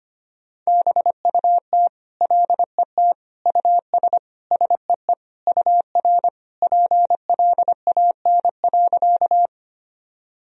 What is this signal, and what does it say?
Morse code, heard as a keyed tone, about 25 wpm, 700 Hz. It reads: BUT LET US SEE UR PLAN.
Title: Sentences from Top 500 Words and Abbreviations 25wpm